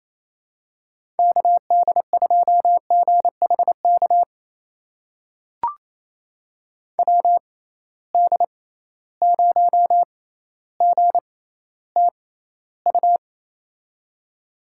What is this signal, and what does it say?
Morse code, heard as a keyed tone, about 27 wpm, spaced as for 14 wpm, 700 Hz. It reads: KD2GHK WD0GTU
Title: USA Call Signs - 6 Characters - 6x Character Spacing - 28wpm